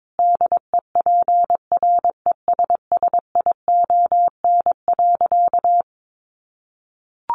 Morse code, heard as a keyed tone, about 22 wpm, 700 Hz. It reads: DEPRESSION.
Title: Sentences from Top 1300 Words 22wpm